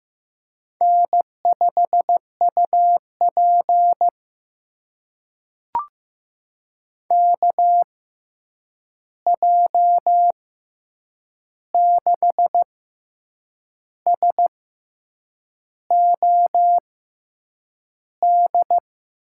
Morse code, heard as a keyed tone, about 15 wpm, spaced as for 7 wpm, 700 Hz. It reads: N5UP KJ6SOD
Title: USA Call Signs - 6x Character Spacing - 15wpm